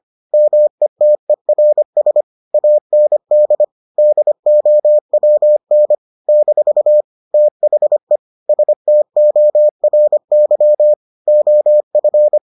Morse code, heard as a keyed tone, about 25 wpm, 600 Hz. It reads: METERS AND DOWN - THE STORY OF